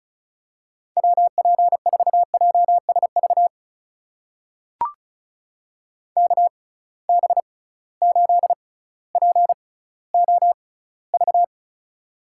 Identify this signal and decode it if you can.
Morse code, heard as a keyed tone, about 36 wpm, spaced as for 18 wpm, 700 Hz. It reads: WP4JSV KB8POV
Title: USA Call Signs - 6 Characters - 6x Character Spacing - 35wpm